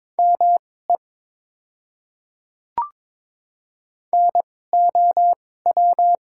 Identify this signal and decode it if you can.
Morse code, heard as a keyed tone, about 22 wpm, spaced as for 16 wpm, 700 Hz. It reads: ME NOW
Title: 2x Character Spacing - Top 200 Words - 22wpm